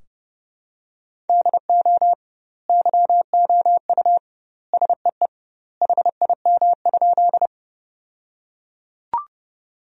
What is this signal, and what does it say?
Morse code, heard as a keyed tone, about 32 wpm, 700 Hz. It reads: DO YOU SEE HIM?
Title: 2x Word-Spacing - Sentences from Top 100 Words - 30wpm